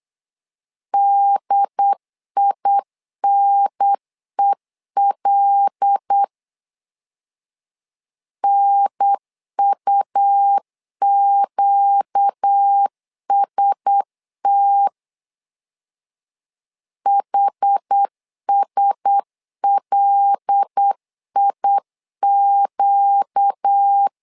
Morse code, heard as a keyed tone, about 8 wpm, 800 Hz. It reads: DINEL NUQST HSLIQ
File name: LEZIONI / 5 LBVC / 5 L 8wpm 2831